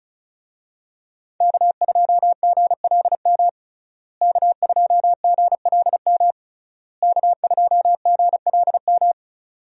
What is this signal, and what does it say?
Morse code, heard as a keyed tone, about 34 wpm, 700 Hz. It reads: K2GLM K2GLM K2GLM
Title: USA Call Signs - 5 Characters - Repeated 3x - 35wpm